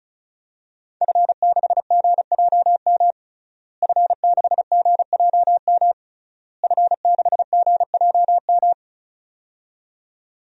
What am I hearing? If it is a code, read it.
Morse code, heard as a keyed tone, about 36 wpm, 700 Hz. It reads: F6GJM F6GJM F6GJM
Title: DXCC-100 Call Signs - 5 Characters - Repeated 3x - 35wpm